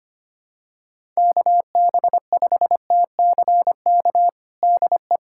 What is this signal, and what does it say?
Morse code, heard as a keyed tone, about 25 wpm, 700 Hz. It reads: KB5TCK DE
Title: USA Callsign DE Callsign - 6 Characters - 25wpm